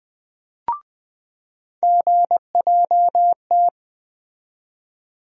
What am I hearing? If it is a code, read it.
Morse code, heard as a keyed tone, about 20 wpm, 700 Hz. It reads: GJT
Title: USA Call Sign Postfixes - 3 Letters 20wpm